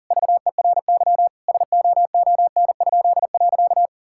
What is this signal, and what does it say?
Morse code, heard as a keyed tone, about 37 wpm, 700 Hz. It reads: VERY SOON?.
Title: Pangram Challenge v2 - 40wpm